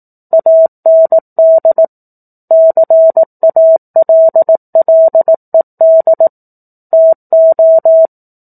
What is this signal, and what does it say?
Morse code, heard as a keyed tone, about 18 wpm, 650 Hz. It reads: AND CALLED TO
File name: 04-Chapter-04 18 WPM